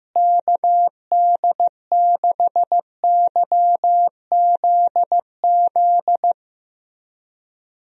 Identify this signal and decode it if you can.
Morse code, heard as a keyed tone, about 15 wpm, 700 Hz. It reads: KD6YZZ
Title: USA Call Signs - 6 Characters - Repeated 3x - 15wpm